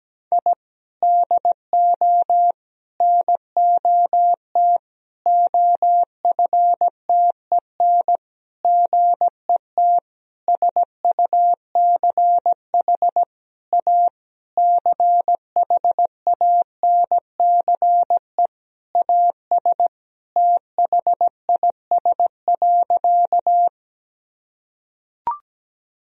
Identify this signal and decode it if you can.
Morse code, heard as a keyed tone, about 17 wpm, 700 Hz. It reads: I DO NOT OFTEN GET SUCH A CHANCE AS THIS.
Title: Sentences from Top 1000 Words - Review Part 1 - 17wpm